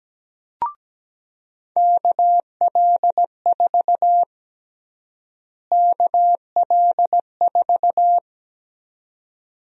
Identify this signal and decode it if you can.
Morse code, heard as a keyed tone, about 17 wpm, 700 Hz. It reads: KL4 KL4
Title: USA Call Sign Prefixes - Repeated 2x - 17wpm